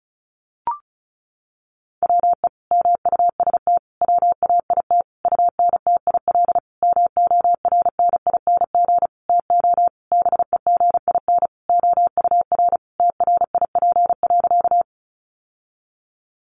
Morse code, heard as a keyed tone, about 34 wpm, 700 Hz. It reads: WE MUST WAIT UNTIL MORNING TO BEGIN OUR TRIP.
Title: Sentences from Top 1300 Words 35wpm